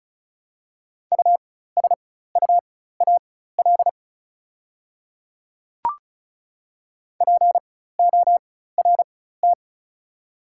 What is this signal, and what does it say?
Morse code, heard as a keyed tone, about 36 wpm, spaced as for 16 wpm, 690 Hz. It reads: USUAL PORT
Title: 4x Character Spacing - Top 500 Words - 35wpm